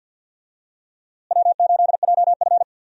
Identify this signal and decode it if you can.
Morse code, heard as a keyed tone, about 46 wpm, 700 Hz. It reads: W8JF
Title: USA Call Signs - 4 Characters - Repeated 2x - 50wpm